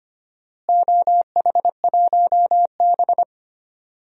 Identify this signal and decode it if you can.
Morse code, heard as a keyed tone, about 25 wpm, 700 Hz. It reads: OH1B
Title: DXCC-100 Call Signs - 4 Characters - Repeated 3x - 25wpm